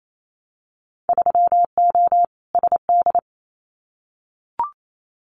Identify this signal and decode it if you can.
Morse code, heard as a keyed tone, about 28 wpm, 700 Hz. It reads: 3O SD
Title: Winter Field Day Exchanges 28wpm